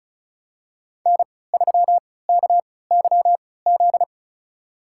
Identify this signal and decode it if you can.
Morse code, heard as a keyed tone, about 34 wpm, spaced as for 20 wpm, 700 Hz. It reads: N3KYZ
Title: USA Call Signs - 5 Characters - 3x Character Spacing - 35wpm